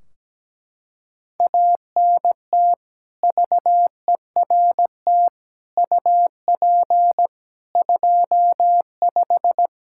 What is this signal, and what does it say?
Morse code, heard as a keyed tone, about 17 wpm, 700 Hz. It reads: ANT VERT UP 25
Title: K1USN SST QSO Practice 17wpm